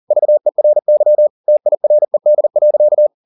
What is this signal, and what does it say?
Morse code, heard as a keyed tone, about 40 wpm, 600 Hz. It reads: VERY TIRED.